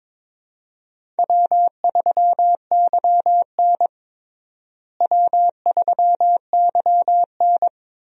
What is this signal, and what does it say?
Morse code, heard as a keyed tone, about 21 wpm, 700 Hz. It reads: W3YN W3YN
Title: USA Call Signs - 4 Characters - Repeated 3x - 22wpm